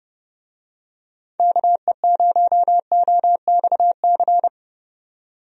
Morse code, heard as a keyed tone, about 31 wpm, 700 Hz. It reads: KE0OXC
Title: USA Call Signs - 6 Characters - 30wpm